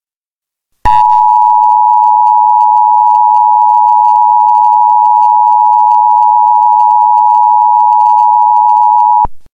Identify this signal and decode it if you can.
Emergency Alert System (SAME) tones (the 853 and 960 Hz attention tone). It is heard from 0.8 to 9.3 s.
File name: Emergency Alert System Eas Tones Attention Signals High Quality